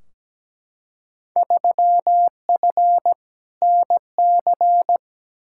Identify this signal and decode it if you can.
Morse code, heard as a keyed tone, about 17 wpm, 700 Hz. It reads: 3F NC
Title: Field Day Exchanges Encore - 17wpm